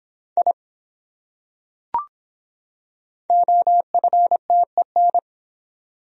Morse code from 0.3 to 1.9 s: I

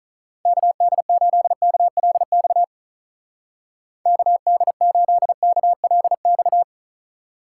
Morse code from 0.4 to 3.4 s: KD8KLX

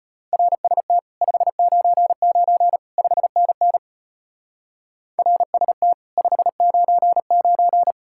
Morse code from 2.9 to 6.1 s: 5NN RST

